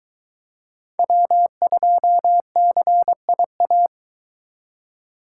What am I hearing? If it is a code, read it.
Morse code, heard as a keyed tone, about 22 wpm, 700 Hz. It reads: W2CIA